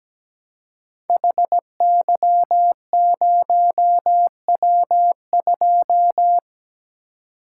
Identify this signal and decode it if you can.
Morse code, heard as a keyed tone, about 17 wpm, 700 Hz. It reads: HY0W2
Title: Code Groups - 5 Letter:Numbers - 17wpm